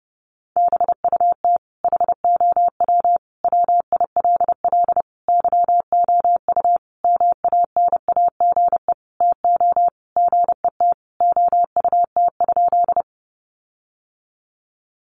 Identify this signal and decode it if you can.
Morse code, heard as a keyed tone, about 30 wpm, 700 Hz. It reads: BUT HOW WILL YOU MANAGE TO GET OUT?
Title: Sentences from Top 1400 Words 30wpm